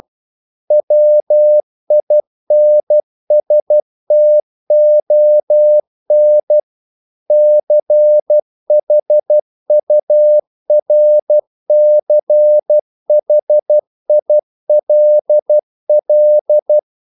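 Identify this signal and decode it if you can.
Morse code, heard as a keyed tone, about 12 wpm, 600 Hz. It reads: WINSTON CHURCHILL
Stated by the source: Morse practice sending English quotations.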